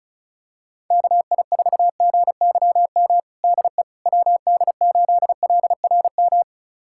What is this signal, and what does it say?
Morse code, heard as a keyed tone, about 36 wpm, 700 Hz. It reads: KI4GYM DE WD8LRM
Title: USA Callsign DE Callsign - 6 Characters - 35wpm